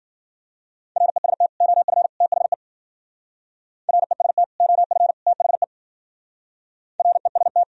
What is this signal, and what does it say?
Morse code, heard as a keyed tone, about 48 wpm, 700 Hz. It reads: REST OF THE REST OF THE REST
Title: Speed-Racing - Sets of 3 Words 40wpm